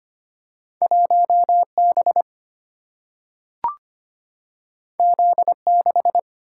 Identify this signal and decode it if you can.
Morse code, heard as a keyed tone, about 25 wpm, 700 Hz. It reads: 1B Z6
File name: Two Letter-Number Combinations - 25wpm